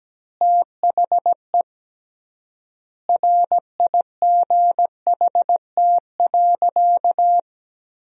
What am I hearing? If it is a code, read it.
Morse code, heard as a keyed tone, about 17 wpm, 700 Hz. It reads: THE RIGHT.
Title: 3x Word-Spacing - Sentences from Top 300 Words - 17wpm